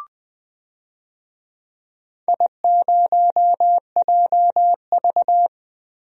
Morse code, heard as a keyed tone, about 20 wpm, 700 Hz. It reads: I0JV
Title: DXCC-100 Call Signs - 4 Characters - Repeated 2x - 20wpm